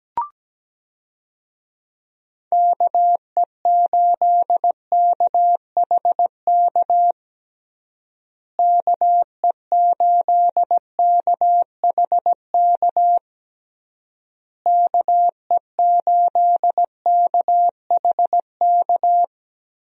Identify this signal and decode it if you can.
Morse code, heard as a keyed tone, about 17 wpm, 700 Hz. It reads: KE8KHK KE8KHK KE8KHK
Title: USA Call Signs - 6 Characters - Repeated 3x - 17wpm